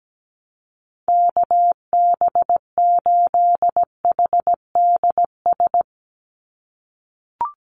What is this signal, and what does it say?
Morse code, heard as a keyed tone, about 17 wpm, 700 Hz. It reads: KB8HDS
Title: USA Call Signs 17wpm